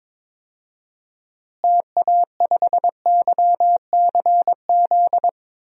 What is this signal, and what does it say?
Morse code, heard as a keyed tone, about 23 wpm, 700 Hz. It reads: TA5YCZ
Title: DXCC-100 Call Signs - 6 Characters - Repeated 3x - 22wpm